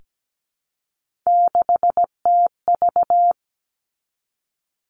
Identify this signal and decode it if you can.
Morse code, heard as a keyed tone, about 17 wpm, 700 Hz. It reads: 6TV
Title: Three Challenging Letter-Number Combinations 17wpm